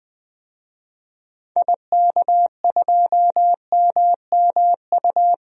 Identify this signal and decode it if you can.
Morse code, heard as a keyed tone, about 20 wpm, 700 Hz. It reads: IK2MMU
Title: DXCC-100 Call Signs - 6 Characters - 20wpm